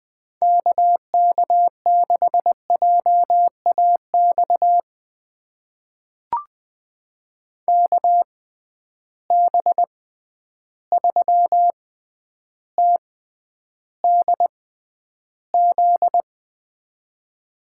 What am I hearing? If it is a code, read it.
Morse code, heard as a keyed tone, about 20 wpm, spaced as for 10 wpm, 700 Hz. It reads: KK6JAX KB3TDZ
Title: USA Call Signs - 6 Characters - 6x Character Spacing - 20wpm